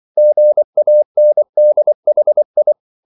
Morse code, heard as a keyed tone, about 24 wpm, 600 Hz. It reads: GANDHI
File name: CW / Practice / EnglishQuotations0224